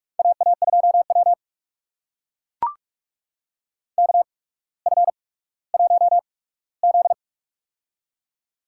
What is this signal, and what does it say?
Morse code, heard as a keyed tone, about 44 wpm, spaced as for 19 wpm, 700 Hz. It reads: AA2W KF1Z
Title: USA Call Signs - 4 Characters - 8x Character Spacing - 45wpm